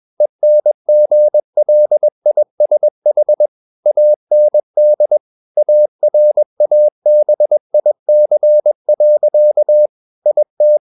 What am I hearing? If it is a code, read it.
Morse code, heard as a keyed tone, about 21 wpm, 600 Hz. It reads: ENGLISH AND ARABIC. IT
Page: CW med Gnister